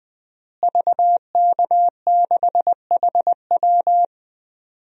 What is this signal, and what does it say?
Morse code, heard as a keyed tone, about 20 wpm, 700 Hz. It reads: VK6HW
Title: DXCC-100 Call Signs - 5 Characters - Repeated 2x - 20wpm